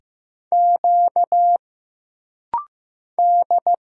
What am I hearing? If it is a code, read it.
Morse code, heard as a keyed tone, about 15 wpm, 700 Hz. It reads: Q D
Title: Single Letters - Rapid-Fire 15wpm